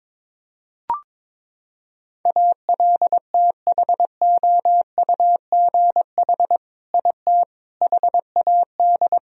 Morse code, heard as a keyed tone, about 22 wpm, 700 Hz. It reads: ALTHOUGH IT HAD
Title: Sentences from Top 1000 Words - Review Part 3 - 22wpm